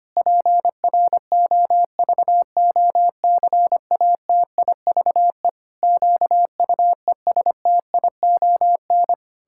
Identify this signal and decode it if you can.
Morse code, heard as a keyed tone, about 25 wpm, 700 Hz. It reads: PROVOCATIVE QUESTION
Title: Pangram Challenge v2 - 25wpm